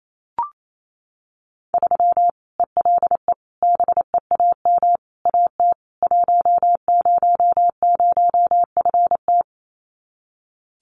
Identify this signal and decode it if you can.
Morse code, heard as a keyed tone, about 28 wpm, 700 Hz. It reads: 3 ELE BEAM AT 100FT
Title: Common QSO Phrases 28wpm